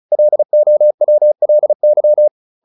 Morse code, heard as a keyed tone, about 34 wpm, 600 Hz. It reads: LOWLY
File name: LY5 35